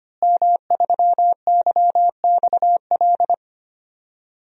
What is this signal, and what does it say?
Morse code, heard as a keyed tone, about 25 wpm, 700 Hz. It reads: M3YXL
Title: DXCC-100 Call Signs - 5 Characters - Repeated 2x - 25wpm